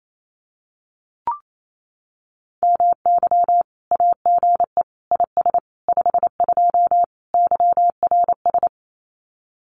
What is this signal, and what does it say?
Morse code, heard as a keyed tone, about 28 wpm, 700 Hz. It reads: MY AGE IS 52 YRS
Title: Common QSO Phrases Encore 28wpm